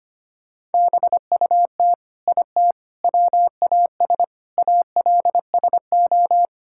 Morse code, heard as a keyed tone, about 25 wpm, 700 Hz. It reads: BUT IT WAS ALSO
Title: Sentences from Top 1600 Words 25wpm